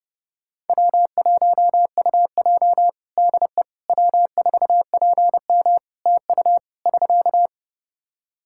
Morse code, heard as a keyed tone, about 30 wpm, 700 Hz. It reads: W1UJ DE W4PM TU <SK>
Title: K1USN SST QSO Practice 30wpm